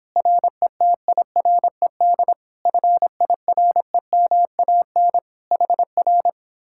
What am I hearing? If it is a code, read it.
Morse code, heard as a keyed tone, about 31 wpm, 700 Hz. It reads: RETIRED FIREMAN HR